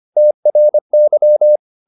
Morse code, heard as a keyed tone, about 25 wpm, 600 Hz. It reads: TRY